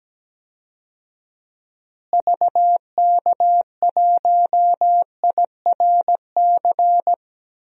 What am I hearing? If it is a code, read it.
Morse code, heard as a keyed tone, about 17 wpm, 700 Hz. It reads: VK1IRC
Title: DXCC-100 Call Signs - 6 Characters - Repeated 2x - 17wpm